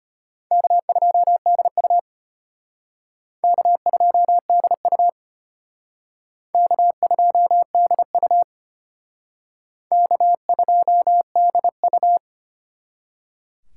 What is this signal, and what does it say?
Morse code, heard as a keyed tone, about 34 wpm, 700 Hz. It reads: K2DU K2DU K2DU K2DU